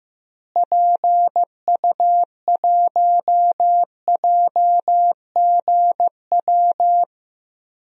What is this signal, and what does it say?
Morse code, heard as a keyed tone, about 16 wpm, 700 Hz. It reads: PU1JGW
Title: DXCC-100 Call Signs - 6 Characters - Repeated 2x - 15wpm